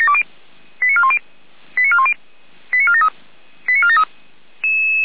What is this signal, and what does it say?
Five-tone selective call (ZVEI), heard as quick runs of five tones, tones of about 70 ms. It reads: 89410 89410 89473 89473